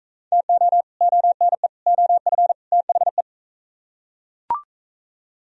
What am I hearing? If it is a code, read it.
Morse code, heard as a keyed tone, about 39 wpm, 700 Hz. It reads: TO ONE OF THE